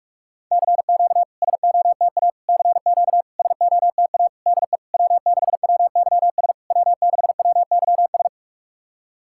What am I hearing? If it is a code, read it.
Morse code, heard as a keyed tone, about 47 wpm, 700 Hz. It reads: CQ SOTA CQ SOTA DE W6WYS W6WYS
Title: SOTA QSO Practice 45wpm